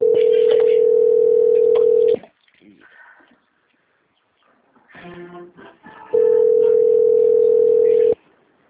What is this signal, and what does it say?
Ringback tone (North America), 440 + 480 Hz, with speech.
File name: out- - - - - .45028